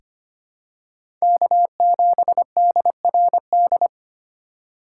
Morse code, heard as a keyed tone, about 25 wpm, 700 Hz. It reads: K7DRD
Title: USA Call Signs - 5 Characters - 2x Character Spacing - 25wpm